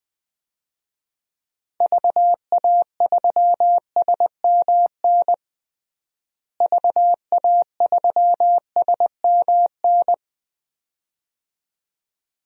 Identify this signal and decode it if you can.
Morse code, heard as a keyed tone, about 21 wpm, 700 Hz. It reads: VA3SMN VA3SMN
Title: DXCC-100 Call Signs - 6 Characters - Repeated 2x - 20wpm